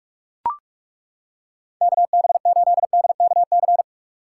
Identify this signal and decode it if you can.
Morse code, heard as a keyed tone, about 36 wpm, 700 Hz. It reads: KB8DKC
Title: Speed Racing - USA 2x3 Call Signs 30wpm